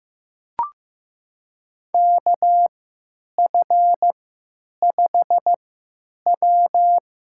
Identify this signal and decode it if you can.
Morse code, heard as a keyed tone, about 16 wpm, spaced as for 9 wpm, 700 Hz. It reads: KF5W
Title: USA Call Signs - 6 Characters - 3x Character Spacing - 15wpm